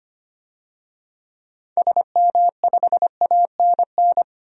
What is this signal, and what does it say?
Morse code, heard as a keyed tone, about 25 wpm, 700 Hz. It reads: SM5ANN
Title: DXCC-100 Call Signs - 6 Characters - 25wpm